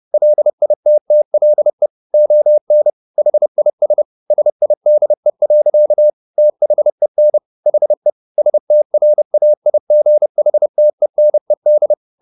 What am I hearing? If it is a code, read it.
Morse code, heard as a keyed tone, about 30 wpm, 600 Hz. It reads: LITTLE ON HIS SIDE. THEN HE STRAIGHTENED